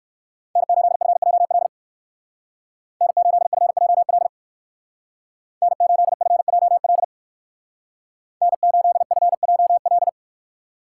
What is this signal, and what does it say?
Morse code, heard as a keyed tone, about 59 wpm, 700 Hz. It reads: N8FJL N8FJL N8FJL N8FJL